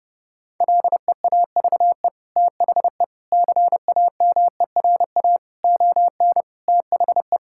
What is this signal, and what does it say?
Morse code, heard as a keyed tone, about 32 wpm, 700 Hz. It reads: LEAVE THE CAMERA ON THE
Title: Sentences from Top 1000 Words - Review Part 5 - 30wpm